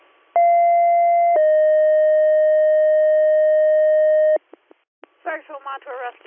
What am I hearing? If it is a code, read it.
Two-tone page, heard as a one-second tone, then a three-second tone, about 690 then 620 Hz, with speech. It is heard from 0.4 to 4.4 s.